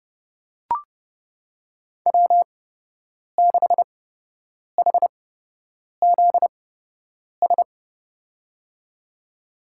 Morse code, heard as a keyed tone, about 30 wpm, spaced as for 8 wpm, 700 Hz. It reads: W6HZS
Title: USA Call Signs - 5 Characters - 8x Character Spacing - 30wpm